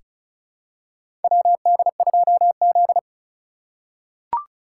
Morse code, heard as a keyed tone, about 36 wpm, 700 Hz. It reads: WD2Z
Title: USA Call Signs - 4 Characters - 3x Character Spacing - 35wpm